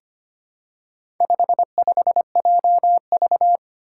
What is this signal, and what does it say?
Morse code, heard as a keyed tone, about 25 wpm, 700 Hz. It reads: 55JV
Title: Four Challenging Letter-Number Combinations 25wpm